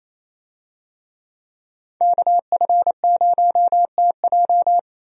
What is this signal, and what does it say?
Morse code, heard as a keyed tone, about 28 wpm, 700 Hz. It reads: KF0TJ
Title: USA Call Signs - 5 Characters - 28wpm